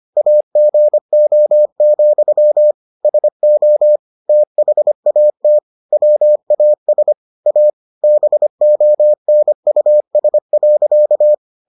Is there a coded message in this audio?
Yes: Morse code from 0.2 to 11.3 s, heard as a keyed tone, about 25 wpm, 600 Hz. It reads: AGO, SO THAT WAS A BONUS.